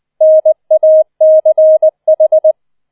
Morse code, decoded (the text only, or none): NACH